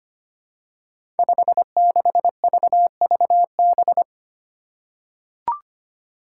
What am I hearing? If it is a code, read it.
Morse code, heard as a keyed tone, about 25 wpm, 700 Hz. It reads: 56VVB
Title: Five Challenging Letter-Number Combinations 25wpm